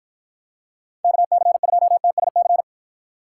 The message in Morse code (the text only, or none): KC2TSC